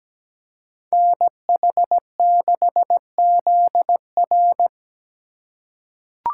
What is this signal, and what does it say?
Morse code, heard as a keyed tone, about 17 wpm, 700 Hz. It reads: NH6ZR
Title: USA Call Signs - 5 Characters - 3x Character Spacing - 17wpm